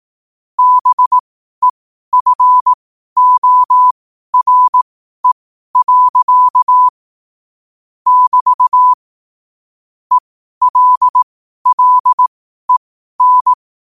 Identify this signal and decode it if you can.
Morse code, heard as a keyed tone, about 18 wpm, spaced as for 12 wpm, 1000 Hz. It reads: BEFORE. <BT> ELLEN